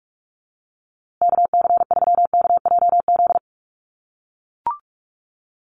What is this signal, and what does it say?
Morse code, heard as a keyed tone, about 44 wpm, 700 Hz. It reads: KC3KJZ
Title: USA Call Signs Repeated 2x 45wpm